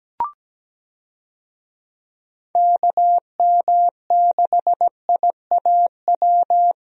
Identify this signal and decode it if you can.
Morse code, heard as a keyed tone, about 17 wpm, 700 Hz. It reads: KM6IAW
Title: USA Call Signs - 6 Characters - Repeated 2x - 17wpm